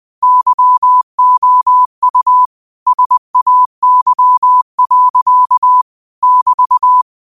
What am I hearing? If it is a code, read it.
Morse code, heard as a keyed tone, about 20 wpm, 1000 Hz. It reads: YOU SAY. <BT>
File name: qotdfiles / 20wpm